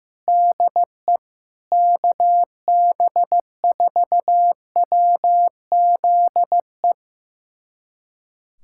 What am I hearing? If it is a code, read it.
Morse code, heard as a keyed tone, about 15 wpm, 700 Hz. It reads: DE KB4WZE
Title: USA Callsign DE Callsign 15wpm